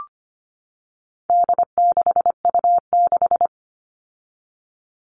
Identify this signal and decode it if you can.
Morse code, heard as a keyed tone, about 25 wpm, 700 Hz. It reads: D6U6
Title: All Cut Numbers - 4-Digit 25wpm